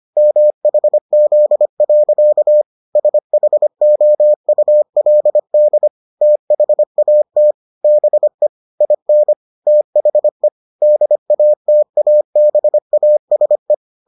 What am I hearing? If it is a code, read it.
Morse code, heard as a keyed tone, about 25 wpm, 600 Hz. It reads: MHZ. SHOULD THAT BE IN THE DATABASE